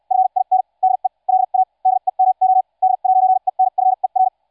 Morse code, heard as a keyed tone, about 74 wpm, spaced as for 52 wpm, 750 Hz. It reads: FIN DE LA DEMO D'ENFER